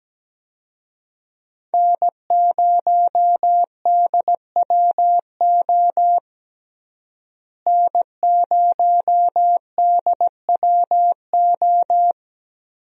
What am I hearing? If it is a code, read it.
Morse code, heard as a keyed tone, about 17 wpm, 700 Hz. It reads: N0DWO N0DWO
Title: USA Call Signs - 5 Characters - Repeated 2x - 17wpm